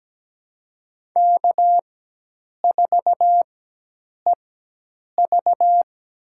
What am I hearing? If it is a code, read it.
Morse code, heard as a keyed tone, about 17 wpm, spaced as for 8 wpm, 700 Hz. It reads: K4EV
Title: USA Call Signs - 4 Characters - 4x Character Spacing - 17wpm